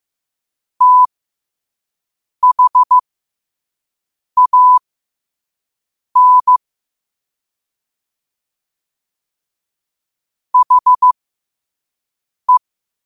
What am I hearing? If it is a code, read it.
Morse code, heard as a keyed tone, about 15 wpm, spaced as for 5 wpm, 1000 Hz. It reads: THAN HE
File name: qotdfiles / 5wpm